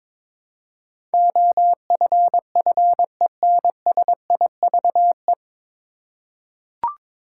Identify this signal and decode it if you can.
Morse code, heard as a keyed tone, about 22 wpm, 700 Hz. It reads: OFFENSIVE
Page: Morse Code Ninja practice 22wpm